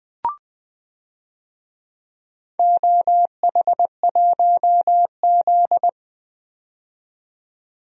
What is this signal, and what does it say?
Morse code, heard as a keyed tone, about 20 wpm, 700 Hz. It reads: OH1Z
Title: DXCC-100 Call Signs - 4 Characters - 20wpm